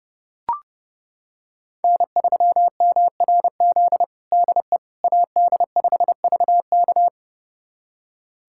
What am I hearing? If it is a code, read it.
Morse code, heard as a keyed tone, about 30 wpm, 700 Hz. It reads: N3MRZ DE AD5VK
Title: USA Callsign DE Callsign - 5 Characters - 30wpm